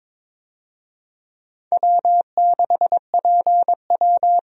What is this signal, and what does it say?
Morse code, heard as a keyed tone, about 21 wpm, 700 Hz. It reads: W6PW
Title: USA Call Signs - 4 Characters - Repeated 2x - 22wpm